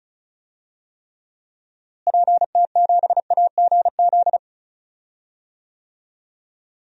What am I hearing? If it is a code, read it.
Morse code, heard as a keyed tone, about 36 wpm, 700 Hz. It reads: PT7AGZ
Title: DXCC-100 Call Signs - 6 Characters - Repeated 3x - 35wpm